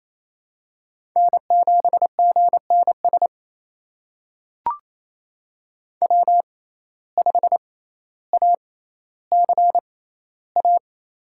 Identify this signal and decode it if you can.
Morse code, heard as a keyed tone, about 27 wpm, spaced as for 14 wpm, 700 Hz. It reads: N7GNS W5ACA
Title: USA Call Signs - 6x Character Spacing - 28wpm